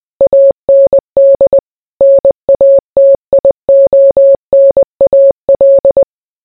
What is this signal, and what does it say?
Morse code, heard as a keyed tone, about 20 wpm, 550 Hz. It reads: AND NATIONAL